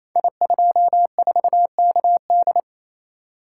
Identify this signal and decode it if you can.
Morse code, heard as a keyed tone, about 29 wpm, 690 Hz. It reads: I24KD
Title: Code Groups - 5 Letter:Numbers - 28wpm